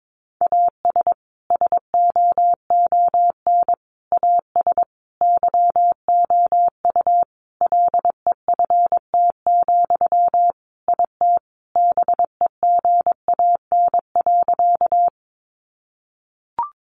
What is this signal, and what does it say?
Morse code, heard as a keyed tone, about 22 wpm, 700 Hz. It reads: AS SOON AS YOU LEFT, IT BEGAN.
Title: Sentences from Top 400 Words 22wpm